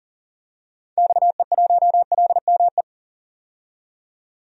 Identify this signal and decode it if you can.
Morse code, heard as a keyed tone, about 37 wpm, 700 Hz. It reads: XE1LME
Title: DXCC-100 Call Signs - 6 Characters - 40wpm